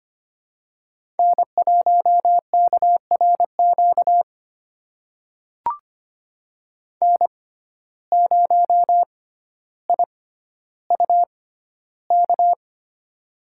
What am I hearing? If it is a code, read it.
Morse code, heard as a keyed tone, about 25 wpm, spaced as for 13 wpm, 700 Hz. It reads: N1KRQ N0IUK
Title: USA Call Signs - 5 Characters - 6x Character Spacing - 25wpm